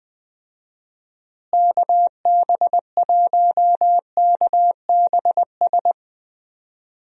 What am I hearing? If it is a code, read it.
Morse code, heard as a keyed tone, about 20 wpm, 700 Hz. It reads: KB1KBS
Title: USA Call Signs - 6 Characters - Repeated 3x - 20wpm